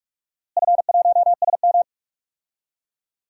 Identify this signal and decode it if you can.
Morse code, heard as a keyed tone, about 42 wpm, 700 Hz. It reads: F1SM
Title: DXCC-100 Call Signs - 4 Characters - Repeated 3x - 45wpm